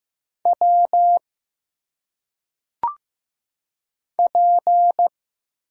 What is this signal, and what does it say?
Morse code, heard as a keyed tone, about 15 wpm, 700 Hz. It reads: W P
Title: Single Letters - Phonetic Pronunciation 15wpm